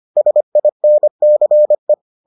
Morse code, heard as a keyed tone, about 25 wpm, 600 Hz. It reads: SINCE